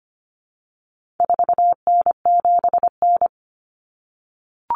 Morse code, heard as a keyed tone, about 25 wpm, 700 Hz. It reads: 4N7N
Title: Common Cut Numbers - 4-Digit 25wpm